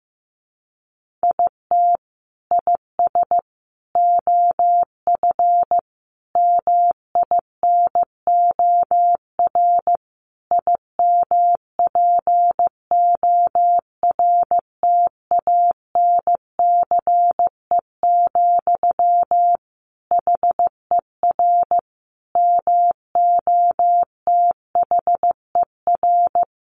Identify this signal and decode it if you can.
Morse code, heard as a keyed tone, about 15 wpm, 700 Hz. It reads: IT IS OF MINOR IMPORTANCE, HER MOTHER
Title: Sentences from Top 1500 Words 15wpm